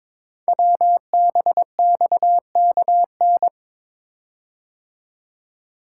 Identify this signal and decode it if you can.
Morse code, heard as a keyed tone, about 22 wpm, 690 Hz. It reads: WBXKN
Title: Code Groups - 5 Letters - 22wpm